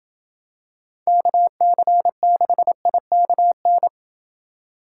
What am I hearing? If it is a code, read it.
Morse code, heard as a keyed tone, about 27 wpm, 700 Hz. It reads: KC6IKN